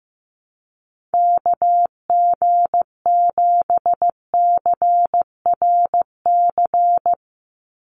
Morse code, heard as a keyed tone, about 15 wpm, 700 Hz. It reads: KG7CRC
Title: USA Call Signs 15wpm